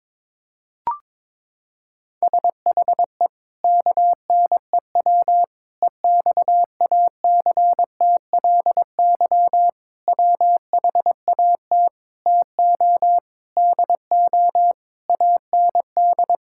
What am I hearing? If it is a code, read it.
Morse code, heard as a keyed tone, about 21 wpm, 700 Hz. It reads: SHE KNEW EXACTLY WHAT TO DO AND
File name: practice / 22wpm / Sentences with 8-Word Phrases - 22wpm